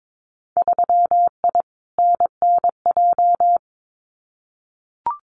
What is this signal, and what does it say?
Morse code, heard as a keyed tone, about 22 wpm, 700 Hz. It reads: 3I NNJ
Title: Winter Field Day Exchanges 22wpm